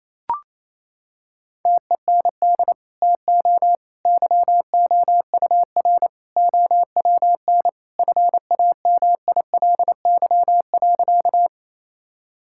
Morse code, heard as a keyed tone, about 27 wpm, 700 Hz. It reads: TEND TO YOUR OWN FAMILY.
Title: Sentences from Top 1000 Words - Review Part 5 - 28wpm